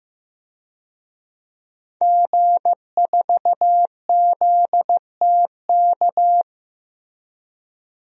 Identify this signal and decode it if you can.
Morse code, heard as a keyed tone, about 15 wpm, 700 Hz. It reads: G4ZTK
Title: DXCC-100 Call Signs - 5 Characters - Repeated 2x - 15wpm